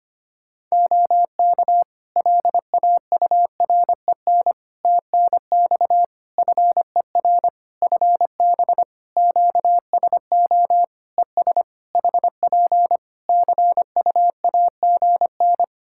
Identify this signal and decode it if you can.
Morse code, heard as a keyed tone, about 25 wpm, 700 Hz. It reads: OK LAUREN TNX FER FB QSO ES HP CUAGN
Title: Practice QSO - Exchange 4 - Ending - 25wpm